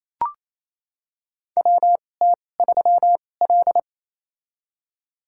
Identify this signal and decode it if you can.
Morse code, heard as a keyed tone, about 28 wpm, spaced as for 20 wpm, 700 Hz. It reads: WT3L
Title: USA Call Signs - 2x Character Spacing - 28wpm